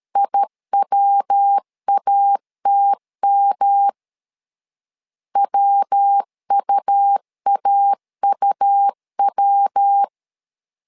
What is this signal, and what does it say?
Morse code, heard as a keyed tone, about 13 wpm, 800 Hz. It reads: IWATM WUAUW
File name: LEZIONI / 3 MWRO / 3 MW 12wpm 1633